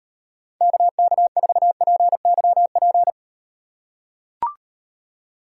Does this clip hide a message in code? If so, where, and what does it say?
Yes: Morse code from 0.6 to 3.1 s, heard as a keyed tone, about 38 wpm, 700 Hz. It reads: KK4PYP